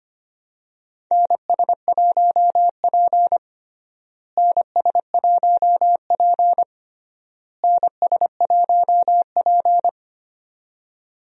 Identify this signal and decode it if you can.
Morse code, heard as a keyed tone, about 25 wpm, 700 Hz. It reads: NS1P NS1P NS1P
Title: USA Call Signs - 4 Characters - Repeated 3x - 25wpm